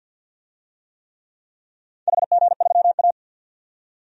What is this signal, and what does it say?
Morse code, heard as a keyed tone, about 47 wpm, 700 Hz. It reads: HG3A
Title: DXCC-100 Call Signs - 4 Characters - 50wpm